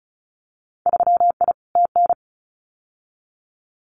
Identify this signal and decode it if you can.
Morse code, heard as a keyed tone, about 34 wpm, 690 Hz. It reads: 3I TN
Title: Winter Field Day Exchanges 35wpm